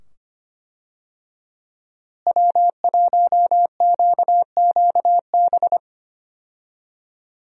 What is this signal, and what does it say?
Morse code, heard as a keyed tone, about 25 wpm, 700 Hz. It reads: W1QQB
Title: USA Call Signs - 5 Characters - Repeated 2x - 25wpm